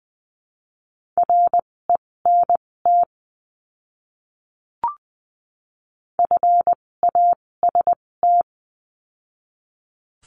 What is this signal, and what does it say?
Morse code, heard as a keyed tone, about 20 wpm, spaced as for 16 wpm, 700 Hz. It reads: RENT FAST
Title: 256 Lesson 42a - Words 12-20wpm